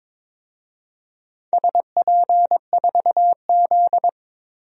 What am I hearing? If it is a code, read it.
Morse code, heard as a keyed tone, about 22 wpm, 700 Hz. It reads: SP4Z
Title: DXCC-100 Call Signs - 4 Characters - 22wpm